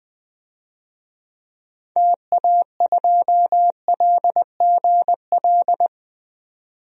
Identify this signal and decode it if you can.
Morse code, heard as a keyed tone, about 20 wpm, 700 Hz. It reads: TA2LGL
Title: DXCC-100 Call Signs - 6 Characters - 20wpm